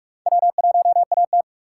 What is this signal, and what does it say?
Morse code, heard as a keyed tone, about 56 wpm, 700 Hz. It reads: W1AT
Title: Speed Racing - USA 1x2 Call Signs 45wpm